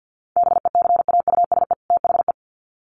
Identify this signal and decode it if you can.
Morse code, heard as a keyed tone, about 50 wpm, 700 Hz. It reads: BECAUSE THE
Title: Sets of 2 Words 50wpm